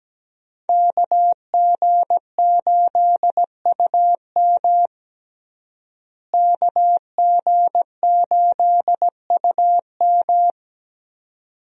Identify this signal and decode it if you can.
Morse code, heard as a keyed tone, about 17 wpm, 700 Hz. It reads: KG8UM KG8UM
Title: USA Call Signs - 5 Characters - Repeated 2x - 17wpm